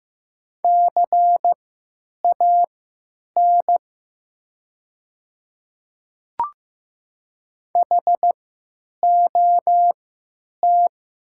Morse code, heard as a keyed tone, about 15 wpm, spaced as for 9 wpm, 700 Hz. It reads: CAN HOT
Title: 3x Character Spacing - Top 200 Words - 15wpm